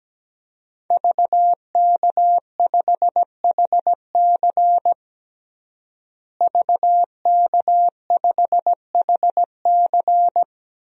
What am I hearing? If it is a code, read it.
Morse code, heard as a keyed tone, about 17 wpm, 700 Hz. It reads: VK5HC VK5HC
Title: DXCC-100 Call Signs - 5 Characters - Repeated 3x - 17wpm